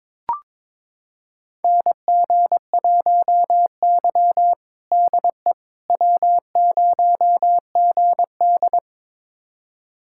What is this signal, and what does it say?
Morse code, heard as a keyed tone, about 22 wpm, 700 Hz. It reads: NG1Y DE W0GD
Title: USA Callsign DE Callsign - 4 Characters - 22wpm